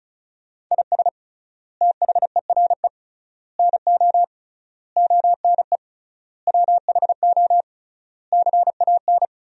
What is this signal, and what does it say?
Morse code, heard as a keyed tone, about 34 wpm, 700 Hz. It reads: IS THERE NO ONE WHO CAN
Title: 3x Word-Spacing - Sentences from Top 200 Words - 35wpm